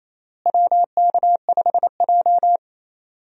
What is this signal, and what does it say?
Morse code, heard as a keyed tone, about 28 wpm, 700 Hz. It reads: WK5J
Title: USA Call Signs - 4 Characters - Repeated 3x - 28wpm